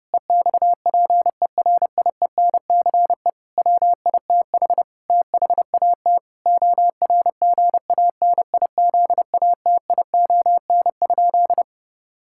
Morse code, heard as a keyed tone, about 30 wpm, 700 Hz. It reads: EXPERIENCE WITH THAT ORGANIZATION?
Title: Sentences from Top 1000 Words - Review Part 5 - 30wpm